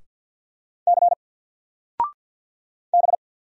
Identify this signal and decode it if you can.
Morse code, heard as a keyed tone, about 50 wpm, 700 Hz. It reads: C B
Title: Single Letter-Number 50wpm